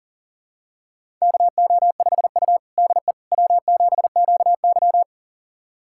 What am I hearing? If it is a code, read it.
Morse code, heard as a keyed tone, about 40 wpm, 700 Hz. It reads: KO5U DE W7QY
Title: USA Callsign DE Callsign - 4 Characters - 40wpm